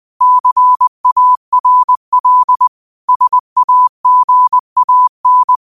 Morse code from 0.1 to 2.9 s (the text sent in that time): CARL